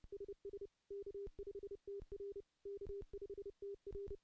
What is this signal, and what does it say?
Morse code, heard as a keyed tone, about 30 wpm, 400 Hz. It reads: SS K5TR K5TR
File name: contests / CW-SS-k5tr- -0600